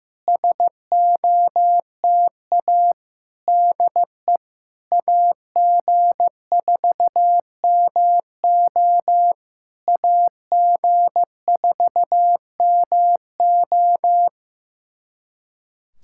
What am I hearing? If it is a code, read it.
Morse code, heard as a keyed tone, about 15 wpm, 700 Hz. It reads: SOTA DE AG4MO AG4MO
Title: SOTA QSO Practice 15wpm